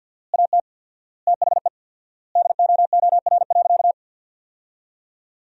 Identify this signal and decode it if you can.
Morse code, heard as a keyed tone, about 50 wpm, 700 Hz. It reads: AT THE DOOR.
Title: 4x Word-Spacing - Sentences from Top 400 Words - 50wpm